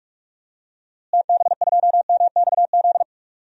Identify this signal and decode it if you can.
Morse code, heard as a keyed tone, about 42 wpm, 700 Hz. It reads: TB2MXZ
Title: DXCC-100 Call Signs - 6 Characters - Repeated 2x - 45wpm